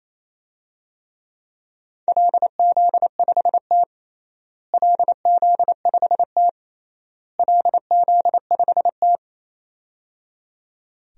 Morse code, heard as a keyed tone, about 28 wpm, 700 Hz. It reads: LZ5T LZ5T LZ5T
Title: DXCC-100 Call Signs - 4 Characters - Repeated 3x - 28wpm